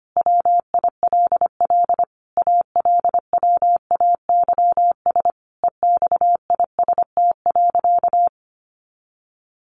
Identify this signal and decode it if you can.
Morse code, heard as a keyed tone, about 25 wpm, 700 Hz. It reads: WILL ALWAYS EXIST.
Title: Sentences from Top 1500 Words 25wpm